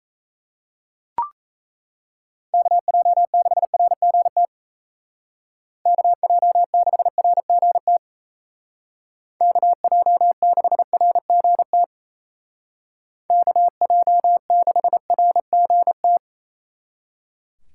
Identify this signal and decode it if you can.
Morse code, heard as a keyed tone, about 36 wpm, 700 Hz. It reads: KJ6RGT KJ6RGT KJ6RGT KJ6RGT